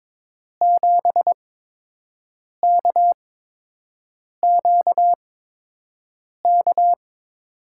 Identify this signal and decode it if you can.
Morse code, heard as a keyed tone, about 22 wpm, spaced as for 6 wpm, 700 Hz. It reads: 7KQK
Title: USA Call Signs - 6 Characters - 8x Character Spacing - 22wpm